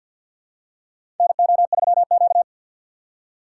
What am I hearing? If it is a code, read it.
Morse code, heard as a keyed tone, about 48 wpm, 700 Hz. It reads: NO3Q
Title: USA Call Signs - 4x Character Spacing - 50wpm